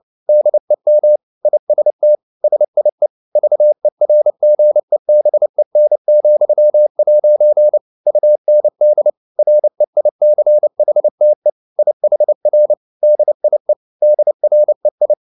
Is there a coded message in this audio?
Yes: Morse code from 0.3 to 15.1 s, heard as a keyed tone, about 29 wpm, 600 Hz. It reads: DEM IST SIE VERGEBEN,' UND REICHTE IHR DIE DREI